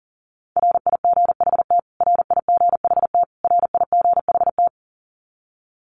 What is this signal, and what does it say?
Morse code, heard as a keyed tone, about 40 wpm, 700 Hz. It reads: RIGHT RIGHT RIGHT
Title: Top 300 Words Repeated 40wpm